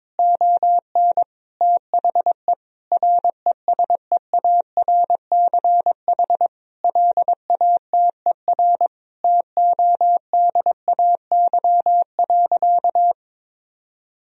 Morse code, heard as a keyed tone, about 22 wpm, 700 Hz. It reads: ON THE RESEARCH LATER TODAY.
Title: Sentences from Top 1000 Words - Review Part 4 - 22wpm